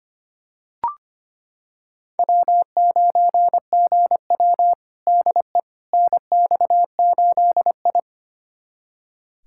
Morse code, heard as a keyed tone, about 25 wpm, 700 Hz. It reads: W9GW DE NX8I
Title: USA Callsign DE Callsign - 4 Characters - 25wpm